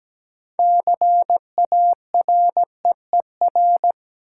Morse code, heard as a keyed tone, about 17 wpm, 700 Hz. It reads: CAREER
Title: Sentences from Top 1000 Words - New Word Drill - 17wpm